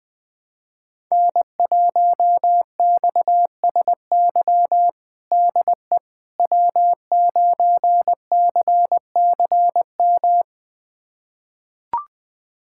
Morse code, heard as a keyed tone, about 20 wpm, 700 Hz. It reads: N1XSY DE W9CCM
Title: USA Callsign DE Callsign - 5 Characters - 20wpm